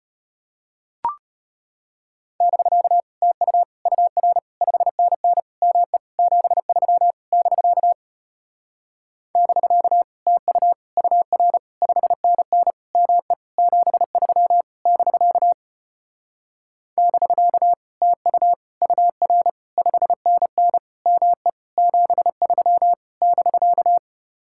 Morse code, heard as a keyed tone, about 33 wpm, 700 Hz. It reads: <BK> TU UR 5NN ME 73 <BK> <BK> TU UR 5NN ME 73 <BK> <BK> TU UR 5NN ME 73 <BK>